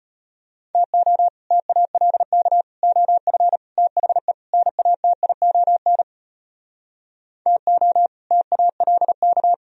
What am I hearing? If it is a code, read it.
Morse code, heard as a keyed tone, about 33 wpm, 700 Hz. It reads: TO TALK OF THE NATION TO TALK